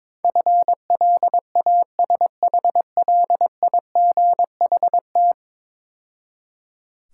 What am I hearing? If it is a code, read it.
Morse code, heard as a keyed tone, about 22 wpm, 690 Hz. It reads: FLASHLIGHT
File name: Easy Words - Speed-Racing with Hints - 22wpm